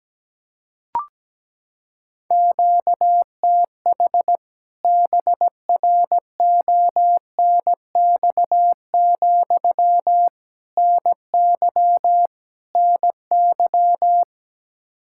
Morse code, heard as a keyed tone, about 17 wpm, 700 Hz. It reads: QTH BRONX, NY NY
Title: K1USN SST QSO Practice 17wpm